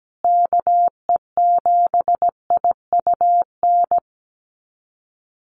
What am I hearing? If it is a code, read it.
Morse code, heard as a keyed tone, about 17 wpm, 700 Hz. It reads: KE7IUN
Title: USA Call Signs Repeated 3x 17wpm